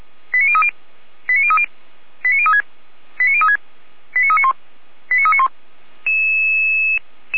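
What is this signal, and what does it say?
Five-tone selective call (ZVEI), heard as quick runs of five tones, tones of about 70 ms. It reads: 89030 89030 89036 89036 89391 89391
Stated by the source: Five-tone ZVEI selective call 89502, received on SDR, from the Feuerwehr Orlamünde (fire brigade) server.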